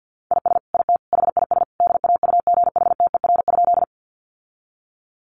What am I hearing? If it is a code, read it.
Morse code, heard as a keyed tone, about 50 wpm, 700 Hz. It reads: IS IT HIS DAUGHTER?
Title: Sentences from Top 1500 Words 50wpm